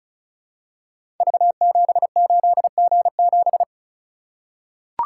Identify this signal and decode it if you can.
Morse code, heard as a keyed tone, about 36 wpm, 700 Hz. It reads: V78G7
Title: Code Groups - 5 Letter:Numbers - 35wpm